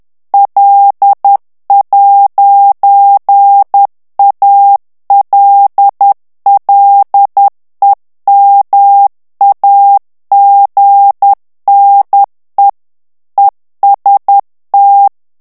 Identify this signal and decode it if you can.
Morse code, heard as a keyed tone, about 10 wpm, 800 Hz. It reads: L'ALLEMAGNE EST
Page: Français - texte en clair 1 - vitesse : 10 mots minute